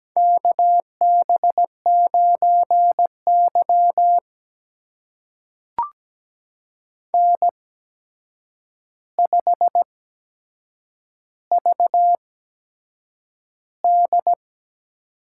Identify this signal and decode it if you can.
Morse code, heard as a keyed tone, about 17 wpm, spaced as for 7 wpm, 700 Hz. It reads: KB9Y N5VD
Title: USA Call Signs - 4 Characters - 8x Character Spacing - 17wpm